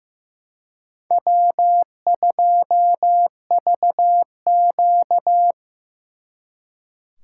Morse code, heard as a keyed tone, about 18 wpm, 700 Hz. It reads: W2VQ
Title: Speed Racing - USA Call Signs 15wpm